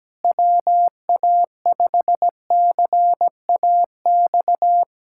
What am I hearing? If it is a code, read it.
Morse code, heard as a keyed tone, about 17 wpm, 700 Hz. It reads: WA5CAX
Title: USA Callsign DE Callsign - 6 Characters - 17wpm